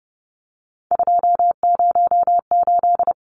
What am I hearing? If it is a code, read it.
Morse code, heard as a keyed tone, about 29 wpm, 700 Hz. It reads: 208
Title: Numbers - 3-Digit 30wpm